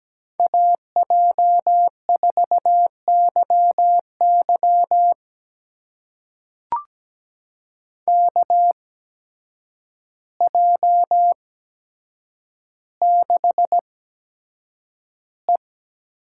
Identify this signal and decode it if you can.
Morse code, heard as a keyed tone, about 17 wpm, spaced as for 8 wpm, 700 Hz. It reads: AJ4YY KJ6E
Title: USA Call Signs - 5 Characters - 8x Character Spacing - 17wpm